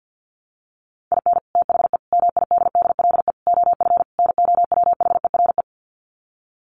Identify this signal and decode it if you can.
Morse code, heard as a keyed tone, about 48 wpm, 700 Hz. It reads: IN THE MIDDLE OF NOWHERE
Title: Sets of 5 Words Encore 50wpm